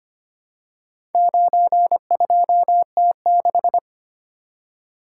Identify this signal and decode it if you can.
Morse code, heard as a keyed tone, about 25 wpm, 700 Hz. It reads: 92T6
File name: practice / 25wpm / Four Letter-Number Combinations - 25wpm